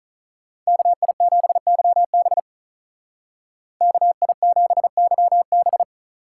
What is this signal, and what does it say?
Morse code, heard as a keyed tone, about 43 wpm, 700 Hz. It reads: KI7YB KI7YB
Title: Speed Racing - USA Call Signs 35wpm